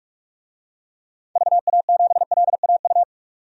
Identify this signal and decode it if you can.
Morse code, heard as a keyed tone, about 41 wpm, 700 Hz. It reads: VA7LAU
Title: DXCC-100 Call Signs - 6 Characters - 45wpm